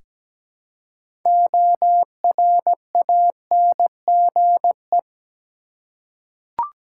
Morse code, heard as a keyed tone, about 17 wpm, 700 Hz. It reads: ORANGE
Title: Six Letter Words - 17wpm